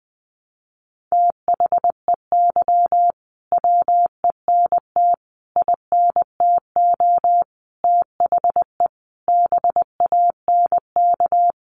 Morse code, heard as a keyed tone, about 20 wpm, 700 Hz. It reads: THEY WENT INTO THE BANK
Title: Sentences from Top 900 Words 20wpm